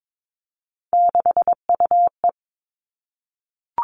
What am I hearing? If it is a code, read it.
Morse code, heard as a keyed tone, about 22 wpm, 700 Hz. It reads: 6UE
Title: Three Challenging Letter-Number Combinations 22wpm